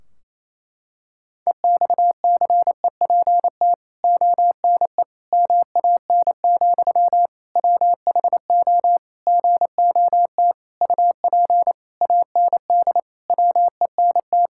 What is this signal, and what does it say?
Morse code, heard as a keyed tone, about 28 wpm, 700 Hz. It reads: EXCEPT ONE MAN, WHO GOT UP AND WENT
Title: Sentences from Top 1600 Words 28wpm